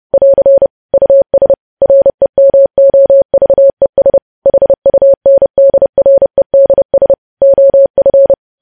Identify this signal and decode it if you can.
Morse code, heard as a keyed tone, about 30 wpm, 550 Hz. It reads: <AR> US REMOVES HUNDREDS OF
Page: news headlines in morse code at 30 wpm